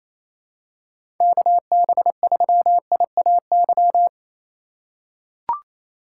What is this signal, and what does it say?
Morse code, heard as a keyed tone, about 29 wpm, 700 Hz. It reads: KB3IAY
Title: USA Call Signs - 3x Character Spacing - 28wpm